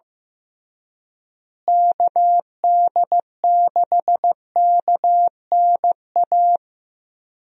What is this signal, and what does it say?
Morse code, heard as a keyed tone, about 15 wpm, 700 Hz. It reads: KD6KNA
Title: USA Call Signs - 6 Characters - Repeated 2x - 15wpm